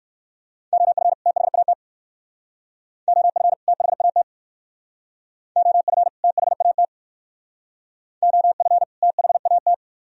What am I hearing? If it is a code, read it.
Morse code, heard as a keyed tone, about 56 wpm, 700 Hz. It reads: OF THAT OF THAT OF THAT OF THAT